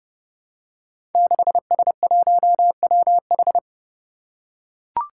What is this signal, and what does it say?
Morse code, heard as a keyed tone, about 30 wpm, 700 Hz. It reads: 6S1WH
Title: Five Challenging Letter-Number Combinations 30wpm